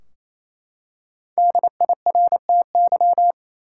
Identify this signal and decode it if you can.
Morse code, heard as a keyed tone, about 28 wpm, 700 Hz. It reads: DIRTY
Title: Five Letter Words - 28wpm